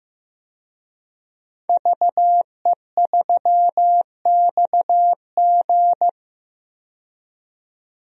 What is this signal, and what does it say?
Morse code, heard as a keyed tone, about 15 wpm, 700 Hz. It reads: VE3XG
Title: DXCC-100 Call Signs - 5 Characters - 15wpm